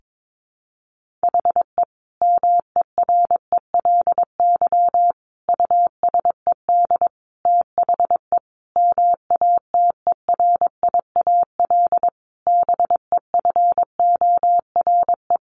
Morse code, heard as a keyed tone, about 22 wpm, 700 Hz. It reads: HE MERELY USED THE MATERIAL BEFORE
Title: Sentences from Top 1400 Words 22wpm